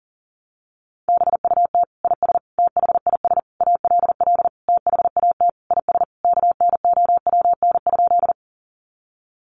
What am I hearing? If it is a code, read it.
Morse code, heard as a keyed tone, about 40 wpm, 700 Hz. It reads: BUT IS THIS ALL THAT IS KNOWN?
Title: Sentences from Top 600 Words 40wpm